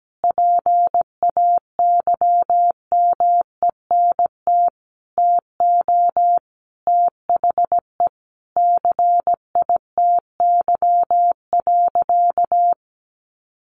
Morse code, heard as a keyed tone, about 17 wpm, 700 Hz. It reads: PAYMENT TO THE CITY.
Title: Sentences from Top 1400 Words 17wpm